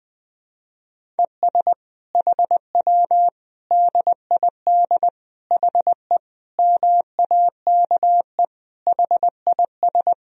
Morse code, heard as a keyed tone, about 20 wpm, 700 Hz. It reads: ES HW DID HE MAKE HIS
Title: Sentences from Top 400 Words and Abbreviations 20wpm